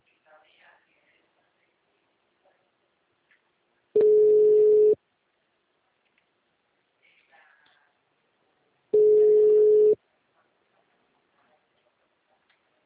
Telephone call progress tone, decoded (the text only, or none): ringback tone